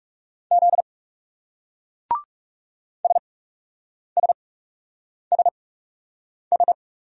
Morse code, heard as a keyed tone, about 38 wpm, 700 Hz. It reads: Z S S S S